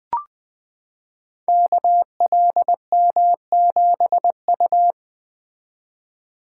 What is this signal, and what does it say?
Morse code, heard as a keyed tone, about 20 wpm, 700 Hz. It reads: KLM7U
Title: Code Groups - 5 Letter:Numbers - 20wpm